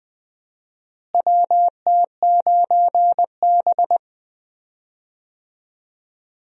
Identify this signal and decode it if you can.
Morse code, heard as a keyed tone, about 20 wpm, 700 Hz. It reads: WT9B
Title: USA Call Signs - 4 Characters - Repeated 3x - 20wpm